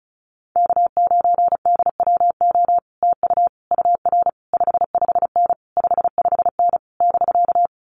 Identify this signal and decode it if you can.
Morse code, heard as a keyed tone, about 36 wpm, 700 Hz. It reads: K9DWO TU UR 55N 55N <BK>
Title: Common QSO Phrases Encore 35wpm